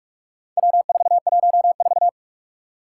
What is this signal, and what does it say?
Morse code, heard as a keyed tone, about 44 wpm, 700 Hz. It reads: W414
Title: Four Challenging Letter-Number Combinations 45wpm